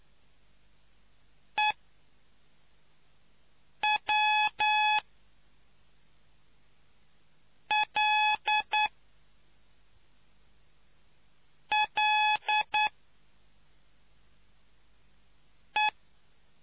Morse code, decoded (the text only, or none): EWLLE